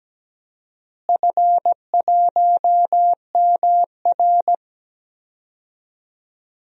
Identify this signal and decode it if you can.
Morse code, heard as a keyed tone, about 17 wpm, 700 Hz. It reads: F1MR
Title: DXCC-100 Call Signs - 4 Characters - Repeated 2x - 17wpm